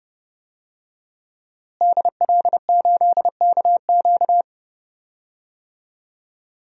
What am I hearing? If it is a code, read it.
Morse code, heard as a keyed tone, about 30 wpm, 700 Hz. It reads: DL8KQ
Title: DXCC-100 Call Signs - 5 Characters - Repeated 3x - 30wpm